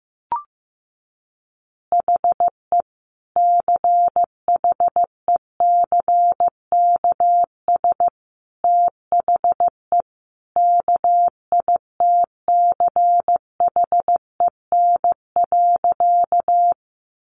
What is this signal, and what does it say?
Morse code, heard as a keyed tone, about 15 wpm, 700 Hz. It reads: HE CHECKS THE KITCHEN.
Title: Easy Phrases 15wpm